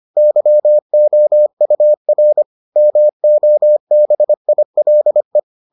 Morse code, heard as a keyed tone, about 25 wpm, 600 Hz. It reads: YOUR MOBILE